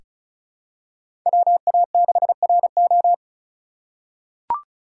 Morse code, heard as a keyed tone, about 34 wpm, 700 Hz. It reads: WA6RO
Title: USA Call Signs - 5 Characters - 4x Character Spacing - 35wpm